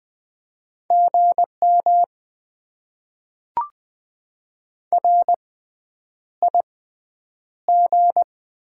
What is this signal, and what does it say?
Morse code, heard as a keyed tone, about 20 wpm, 700 Hz. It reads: G M R I G
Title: Top 20 QSO Elements - 6x Character Spacing - 20wpm